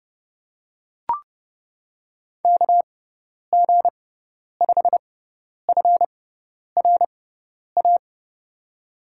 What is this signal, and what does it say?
Morse code, heard as a keyed tone, about 32 wpm, spaced as for 10 wpm, 700 Hz. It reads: KG5FRA
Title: USA Call Signs - 6 Characters - 6x Character Spacing - 30wpm